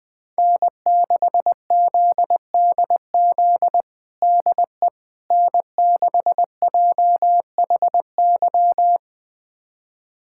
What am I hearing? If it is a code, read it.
Morse code, heard as a keyed tone, about 20 wpm, 700 Hz. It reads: N6ZDZ DE N6JHY
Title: USA Callsign DE Callsign - 5 Characters - 20wpm